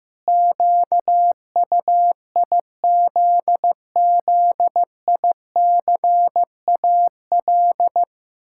Morse code, heard as a keyed tone, about 15 wpm, 700 Hz. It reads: QUIZZICAL
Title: Pangram Challenge v2 - 15wpm